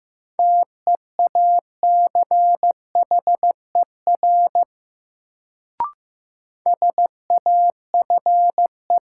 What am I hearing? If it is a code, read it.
Morse code, heard as a keyed tone, about 15 wpm, 700 Hz. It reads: TEACHER SAFE
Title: Sentences from Top 900 Words - New Word Drill - 15wpm